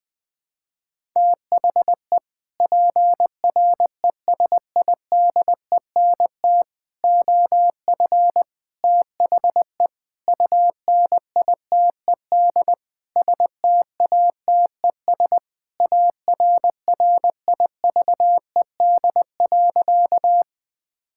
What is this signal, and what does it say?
Morse code, heard as a keyed tone, about 21 wpm, 700 Hz. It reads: THE PRESIDENT OF THE UNITED STATES ARRIVED.
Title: Sentences with 6-Word Phrases - 20wpm